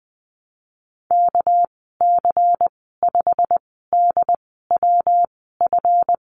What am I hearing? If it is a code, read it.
Morse code, heard as a keyed tone, about 20 wpm, spaced as for 15 wpm, 700 Hz. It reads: KC5DWF
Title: 200 Lesson 32g - Letter k - Call Signs 20wpm